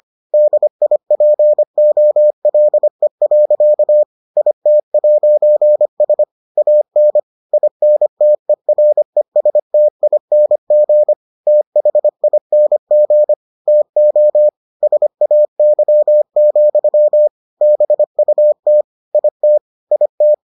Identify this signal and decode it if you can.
Morse code, heard as a keyed tone, about 25 wpm, 600 Hz. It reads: DIPOLE. IT'S AN INTERESTING THING TO SAY, BUT IT IT